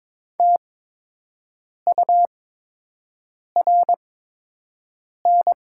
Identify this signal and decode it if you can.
Morse code, heard as a keyed tone, about 22 wpm, spaced as for 6 wpm, 700 Hz. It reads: TURN
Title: 8x Character Spacing - Top 200 Words - 22wpm